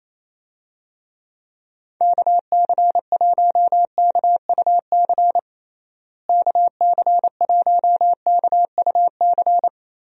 Morse code, heard as a keyed tone, about 27 wpm, 700 Hz. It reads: KC1KUC KC1KUC
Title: USA Call Signs - 6 Characters - Repeated 3x - 28wpm